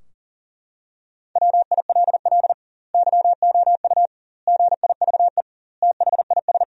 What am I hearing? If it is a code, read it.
Morse code, heard as a keyed tone, about 40 wpm, 700 Hz. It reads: WILL YOU GIVE THIS
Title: 2x Word-Spacing - Sentences from Top 200 Words - 40wpm